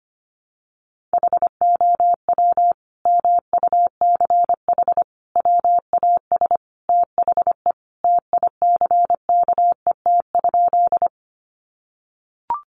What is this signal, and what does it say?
Morse code, heard as a keyed tone, about 25 wpm, 700 Hz. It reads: HOW MUCH WAS THE TICKET?
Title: Sentences from Top 1400 Words 25wpm